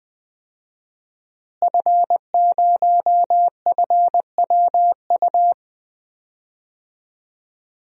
Morse code, heard as a keyed tone, about 20 wpm, 700 Hz. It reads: F0FWU
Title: DXCC-100 Call Signs - 5 Characters - Repeated 2x - 20wpm